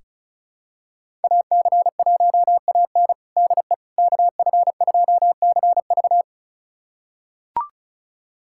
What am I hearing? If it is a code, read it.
Morse code, heard as a keyed tone, about 36 wpm, 700 Hz. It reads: AC1AN DE KF2CV
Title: USA Callsign DE Callsign - 5 Characters - 35wpm